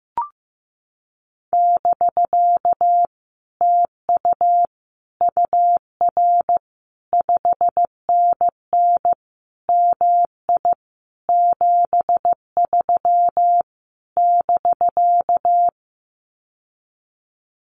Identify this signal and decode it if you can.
Morse code, heard as a keyed tone, about 16 wpm, 700 Hz. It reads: <BK> TU UR 5NN MI 73 <BK>
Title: Common QSO Phrases Encore 15wpm